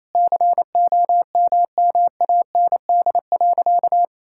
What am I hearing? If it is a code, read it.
Morse code, heard as a keyed tone, about 28 wpm, 700 Hz. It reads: COMMAND.
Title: Sentences from Top 1600 Words 28wpm